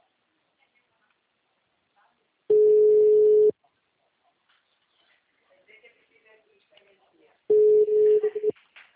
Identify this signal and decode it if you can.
Ringback tone (Europe, 425 Hz), with speech.